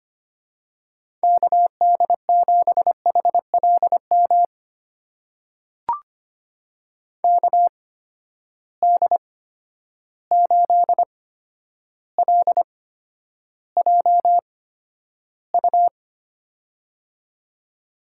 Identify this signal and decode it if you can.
Morse code, heard as a keyed tone, about 25 wpm, spaced as for 11 wpm, 700 Hz. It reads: KD7HLM KD8LJU
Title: USA Call Signs - 8x Character Spacing - 25wpm